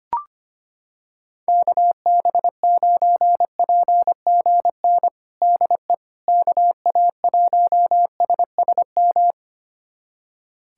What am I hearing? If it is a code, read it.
Morse code, heard as a keyed tone, about 25 wpm, 700 Hz. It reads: KB9PGN DE KA1SSM
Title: USA Callsign DE Callsign - 6 Characters - 25wpm